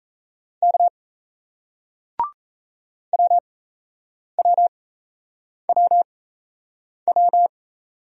Morse code, heard as a keyed tone, about 37 wpm, spaced as for 8 wpm, 700 Hz. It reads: K WWWW